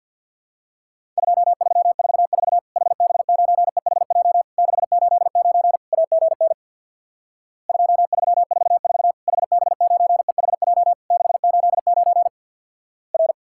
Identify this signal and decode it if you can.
Morse code, heard as a keyed tone, about 50 wpm, 700 Hz. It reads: 2344 HB9EHJ 689 AGN 2344 HB9EHJ 689 R
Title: Worked All Europe DX Contest - QTC Exchange with Repeats - 50wpm